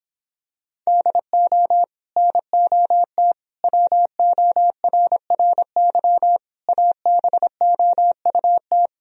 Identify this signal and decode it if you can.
Morse code, heard as a keyed tone, about 26 wpm, 700 Hz. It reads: DO NOT WORRY ABOUT